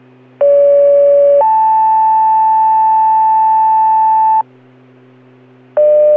Two-tone page, heard as a one-second tone, then a three-second tone, about 570 then 880 Hz. It is heard from 0.4 to 4.4 s.